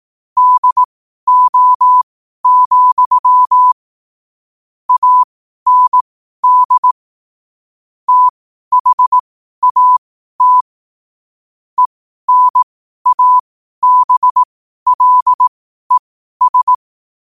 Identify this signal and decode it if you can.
Morse code, heard as a keyed tone, about 18 wpm, spaced as for 12 wpm, 1000 Hz. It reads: DO, AND THAT ENABLES
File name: qotdfiles / 12wpm